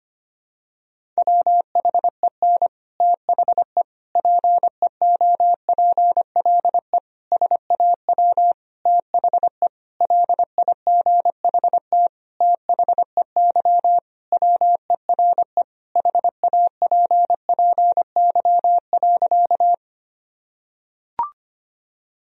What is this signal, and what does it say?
Morse code, heard as a keyed tone, about 25 wpm, 700 Hz. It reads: WHEN THE PEOPLE SAW THE LIGHT THEY WERE HAPPY.
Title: Sentences from Top 1000 Words - Review Part 1 - 25wpm